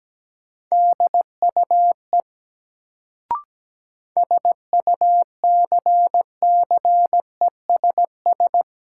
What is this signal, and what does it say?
Morse code, heard as a keyed tone, about 17 wpm, 700 Hz. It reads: DUE SUCCESS
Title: Sentences from Top 900 Words - New Word Drill - 17wpm